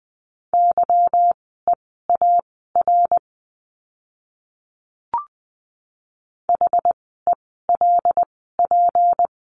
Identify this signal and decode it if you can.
Morse code, heard as a keyed tone, about 20 wpm, spaced as for 14 wpm, 700 Hz. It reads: YEAR HELP
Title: 192 Lesson 31f - Letter v - Words 20wpm